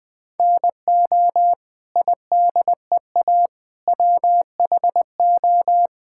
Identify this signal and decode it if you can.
Morse code, heard as a keyed tone, about 21 wpm, 700 Hz. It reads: NO IDEA WHO
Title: Sets of 9 Words - 20wpm